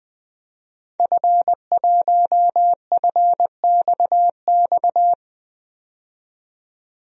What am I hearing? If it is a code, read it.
Morse code, heard as a keyed tone, about 21 wpm, 700 Hz. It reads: F1FXX
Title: DXCC-100 Call Signs - 5 Characters - 20wpm